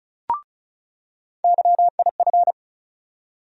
Morse code, heard as a keyed tone, about 36 wpm, 700 Hz. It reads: YIF
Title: USA Call Sign Postfixes - 3 Letters 35wpm